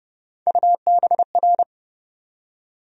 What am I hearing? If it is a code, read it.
Morse code, heard as a keyed tone, about 31 wpm, 700 Hz. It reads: UBR
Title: USA Call Sign Postfixes - 3 Letters 30wpm